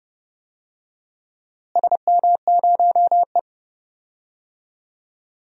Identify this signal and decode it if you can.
Morse code, heard as a keyed tone, about 29 wpm, 700 Hz. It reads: SM0E
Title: DXCC-100 Call Signs - 4 Characters - Repeated 3x - 30wpm